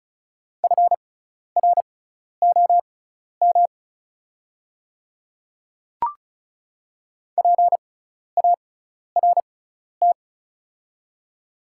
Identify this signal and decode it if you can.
Morse code, heard as a keyed tone, about 36 wpm, spaced as for 12 wpm, 700 Hz. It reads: FROM PART
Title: 6x Character Spacing - Top 200 Words - 35wpm